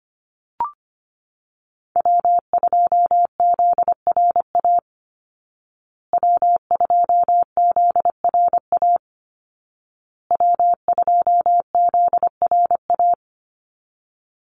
Morse code, heard as a keyed tone, about 25 wpm, 700 Hz. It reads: W2ZRA W2ZRA W2ZRA
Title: Call Signs Repeated 2x - 25wpm